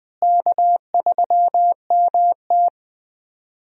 Morse code, heard as a keyed tone, about 20 wpm, 700 Hz. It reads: K3MT
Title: USA Call Signs - 4 Characters - Repeated 2x - 20wpm